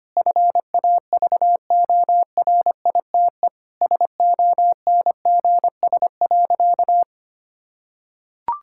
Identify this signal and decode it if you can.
Morse code, heard as a keyed tone, about 25 wpm, 700 Hz. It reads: FAVORITE SONGS.
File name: practice / 25wpm / Sentences with 4-Word Phrases - 25wpm